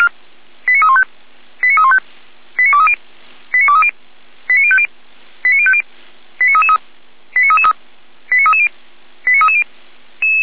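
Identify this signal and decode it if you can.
Five-tone selective call (ZVEI), heard as quick runs of five tones, tones of about 70 ms. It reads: 89315 89315 89230 89230 89050 89050 89333 89333 89330 89330
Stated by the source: ZVEI five-tone selective call received by SDR, feuerwehr-orlamuende.org fire brigade.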